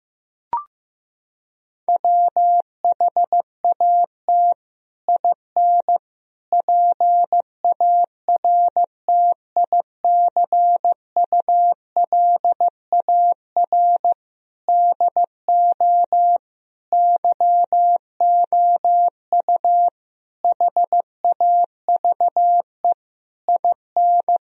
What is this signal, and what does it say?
Morse code, heard as a keyed tone, about 15 wpm, 700 Hz. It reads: WHAT IN PARTICULAR DO YOU HAVE IN
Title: Sentences from Top 1000 Words - Review Part 4 - 15wpm